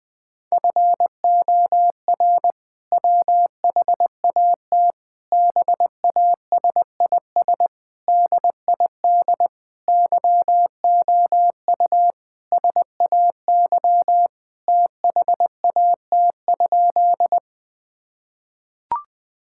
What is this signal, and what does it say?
Morse code, heard as a keyed tone, about 21 wpm, 700 Hz. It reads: FOR WHAT BASIS DID YOU SAY THAT?
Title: Sentences from Top 1000 Words - Review Part 5 - 20wpm